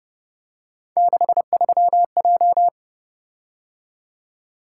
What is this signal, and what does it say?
Morse code, heard as a keyed tone, about 30 wpm, 700 Hz. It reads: 63J